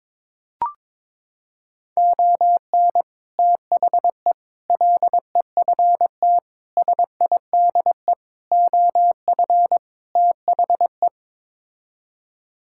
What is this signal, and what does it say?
Morse code, heard as a keyed tone, about 22 wpm, 700 Hz. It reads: ON THE LEFT SIDE OF THE
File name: practice / 22wpm / Sets of 6 Words - 22wpm